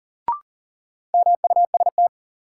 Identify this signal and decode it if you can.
Morse code, heard as a keyed tone, about 40 wpm, 700 Hz. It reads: MUST
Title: Sentences from Top 200 Words - New Word Drill - 40wpm